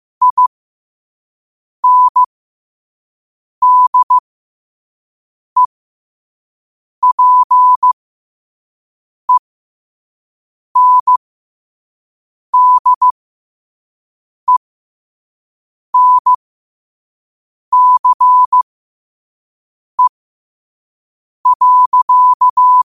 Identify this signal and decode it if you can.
Morse code, heard as a keyed tone, about 15 wpm, spaced as for 5 wpm, 1000 Hz. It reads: INDEPENDENCE.
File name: qotdfiles / 5wpm